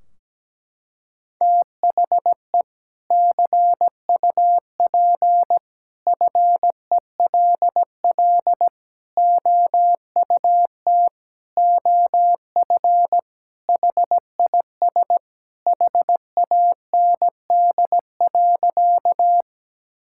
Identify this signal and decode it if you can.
Morse code, heard as a keyed tone, about 17 wpm, 700 Hz. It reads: THE CUP FELL OUT OF HIS HAND.
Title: Sentences from Top 1600 Words 17wpm